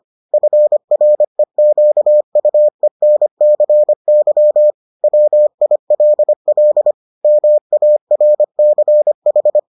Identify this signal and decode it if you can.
Morse code, heard as a keyed tone, about 25 wpm, 600 Hz. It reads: FREQUENCY WILL MARCH